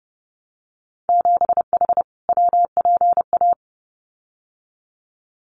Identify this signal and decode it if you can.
Morse code, heard as a keyed tone, about 30 wpm, 700 Hz. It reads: 7H WPA
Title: Winter Field Day Exchanges Encore 30wpm